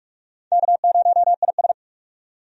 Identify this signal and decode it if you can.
Morse code, heard as a keyed tone, about 43 wpm, 700 Hz. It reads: K0IS
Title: USA Call Signs - 4 Characters - 45wpm